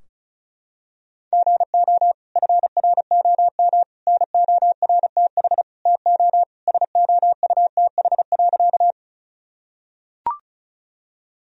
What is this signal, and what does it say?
Morse code, heard as a keyed tone, about 36 wpm, 700 Hz. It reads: GO FROM NORTH TO SOUTH.
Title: Sentences from Top 1000 Words - Review Part 2 - 35wpm